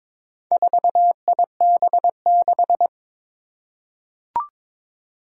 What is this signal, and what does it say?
Morse code, heard as a keyed tone, about 22 wpm, 700 Hz. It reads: 4IB6
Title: Four Challenging Letter-Number Combinations 22wpm